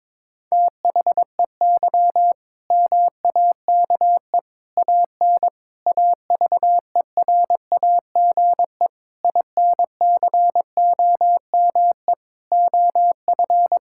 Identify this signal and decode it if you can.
Morse code, heard as a keyed tone, about 21 wpm, 700 Hz. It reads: THEY MAKE AN AVERAGE INCOME OF
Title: Sentences from Top 1000 Words - Review Part 6 - 22wpm